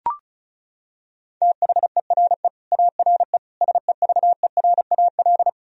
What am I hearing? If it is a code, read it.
Morse code, heard as a keyed tone, about 36 wpm, 700 Hz. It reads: THERE ARE SEVERAL
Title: Sentences from Top 1000 Words - Review Part 2 - 35wpm